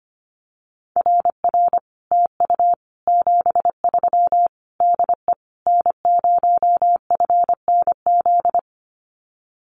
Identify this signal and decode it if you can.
Morse code, heard as a keyed tone, about 25 wpm, 700 Hz. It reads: RR TU 73 DE N0FNZ
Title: Common QSO Phrases Encore 25wpm